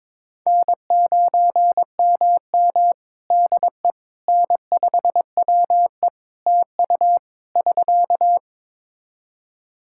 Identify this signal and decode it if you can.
Morse code, heard as a keyed tone, about 22 wpm, 700 Hz. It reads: N9MM DE N5WE TU <SK>
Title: Practice QSO - Exchange 4 - Ending - 22wpm